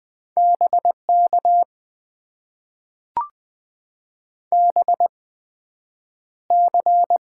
Morse code, heard as a keyed tone, about 20 wpm, spaced as for 9 wpm, 700 Hz. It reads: BK BC
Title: Top 90 QSO Elements - 8x Character Spacing - 20wpm